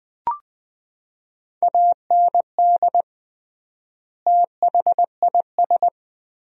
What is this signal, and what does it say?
Morse code, heard as a keyed tone, about 20 wpm, 700 Hz. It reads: AND THIS
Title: 3x Word-Spacing - Sentences from Top 300 Words - 20wpm